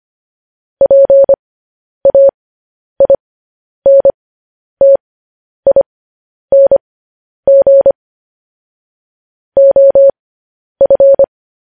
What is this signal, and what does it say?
Morse code, heard as a keyed tone, about 25 wpm, spaced as for 10 wpm, 550 Hz. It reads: PAINTING OF
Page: news-headlines-in-morse-code-at-10-wpm